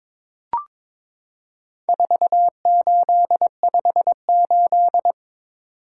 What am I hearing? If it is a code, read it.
Morse code, heard as a keyed tone, about 22 wpm, 700 Hz. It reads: 4858
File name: practice / 22wpm / Four Letter-Number Combinations - 22wpm